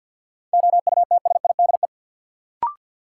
Morse code, heard as a keyed tone, about 53 wpm, 700 Hz. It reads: OUTSIDE
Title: Sentences from Top 900 Words - New Word Drill - 50wpm